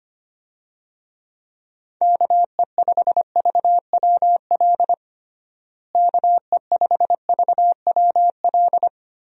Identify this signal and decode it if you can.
Morse code, heard as a keyed tone, about 25 wpm, 700 Hz. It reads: KE5VWL KE5VWL
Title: USA Call Signs - 6 Characters - Repeated 2x - 25wpm